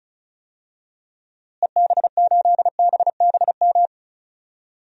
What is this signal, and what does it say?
Morse code, heard as a keyed tone, about 36 wpm, 700 Hz. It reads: EB8BBM
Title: DXCC-100 Call Signs - 6 Characters - Repeated 2x - 35wpm